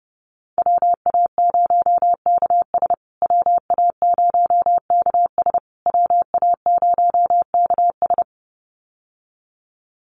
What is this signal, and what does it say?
Morse code, heard as a keyed tone, about 28 wpm, 700 Hz. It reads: WA0KS WA0KS WA0KS
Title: USA Call Signs Repeated 3x 30wpm